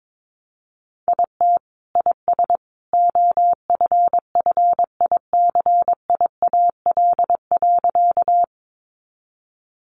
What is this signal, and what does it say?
Morse code, heard as a keyed tone, about 22 wpm, 700 Hz. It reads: IT IS OFFICIAL.
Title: Sentences from Top 1000 Words 22wpm